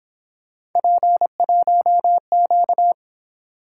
Morse code, heard as a keyed tone, about 26 wpm, 700 Hz. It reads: P1Q
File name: Speed-Racing - Three Letter-Number Combinations - 22wpm